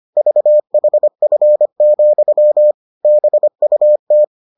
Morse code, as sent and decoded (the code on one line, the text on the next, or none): ...- .... ..-. --..-- / -... ..- -
VHF, BUT